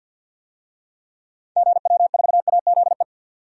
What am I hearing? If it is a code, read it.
Morse code, heard as a keyed tone, about 47 wpm, 700 Hz. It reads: GW4AZE
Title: DXCC-100 Call Signs - 6 Characters - 50wpm